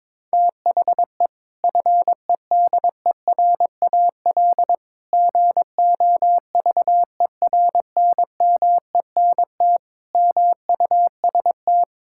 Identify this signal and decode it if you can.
Morse code, heard as a keyed tone, about 22 wpm, 700 Hz. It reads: THE FEDERAL GOVERNMENT MUST
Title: Sentences from Top 1000 Words - Review Part 5 - 22wpm